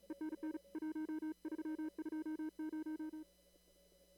Morse code, heard as a keyed tone, about 37 wpm, 310 Hz. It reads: ENN 1320
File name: CW / kt5j wpx cw-op-k5pi